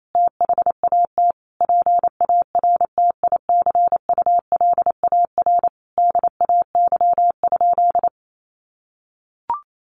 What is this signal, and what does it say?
Morse code, heard as a keyed tone, about 28 wpm, 700 Hz. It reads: THAT PARTICULAR DAY?
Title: Sentences from Top 1400 Words 28wpm